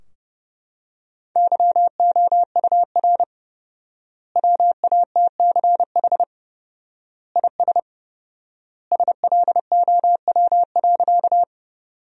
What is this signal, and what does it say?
Morse code, heard as a keyed tone, about 30 wpm, 700 Hz. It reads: YOUR WATCH IS SLOW.
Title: 4x Word-Spacing - Sentences from Top 400 Words - 30wpm